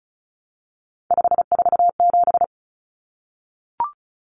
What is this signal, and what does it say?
Morse code, heard as a keyed tone, about 34 wpm, 700 Hz. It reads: 547
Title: Common Cut Numbers - 3-Digit 35wpm